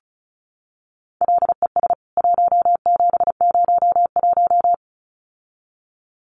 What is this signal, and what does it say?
Morse code, heard as a keyed tone, about 34 wpm, 700 Hz. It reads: LES 1701
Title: CWT Exchange 35wpm